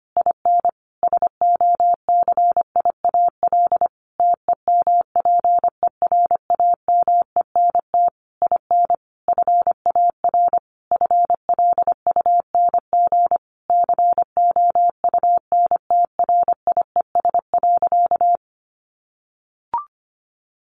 Morse code, heard as a keyed tone, about 25 wpm, 700 Hz. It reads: IN SOCIAL TEMPERAMENT IN FAR FLUNG COUNTRIES.
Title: A Sound Of Thunder 25wpm